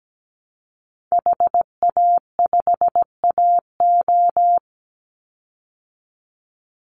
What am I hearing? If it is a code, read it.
Morse code, heard as a keyed tone, about 17 wpm, 700 Hz. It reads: HA5AO
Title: Call Signs Repeated 3x - 17wpm